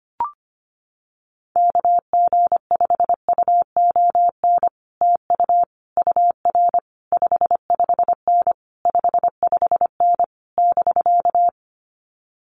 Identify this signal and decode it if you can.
Morse code, heard as a keyed tone, about 25 wpm, 700 Hz. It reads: KG5UON TU UR 55N 55N <BK>
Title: Common QSO Phrases Encore 25wpm